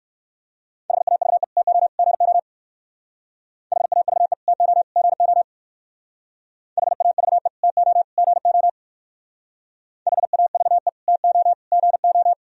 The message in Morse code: HAVE TO GO HAVE TO GO HAVE TO GO HAVE TO GO

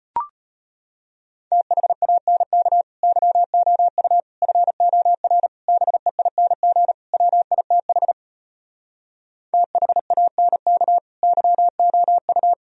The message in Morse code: THANK YOU FOR BEING WITH THANK YOU

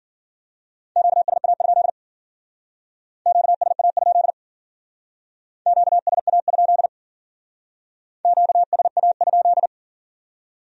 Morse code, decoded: QSA? QSA? QSA? QSA?

